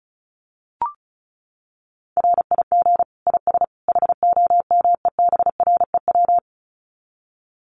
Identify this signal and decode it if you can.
Morse code, heard as a keyed tone, about 34 wpm, 700 Hz. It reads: RIG IS HOMEBREW
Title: Common QSO Phrases 35wpm